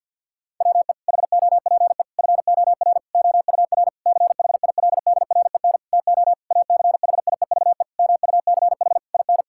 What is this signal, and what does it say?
Morse code, heard as a keyed tone, about 52 wpm, 700 Hz. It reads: WE HOPE FOR OUR CHILDREN TO ACHIEVE MUCH IN
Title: Sentences from Top 1000 Words - Review Part 6 - 50wpm